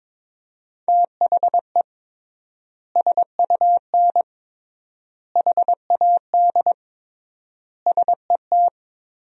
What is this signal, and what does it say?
Morse code, heard as a keyed tone, about 22 wpm, 700 Hz. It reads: THE SUN HAD SET
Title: 3x Word-Spacing - Sentences from Top 300 Words - 22wpm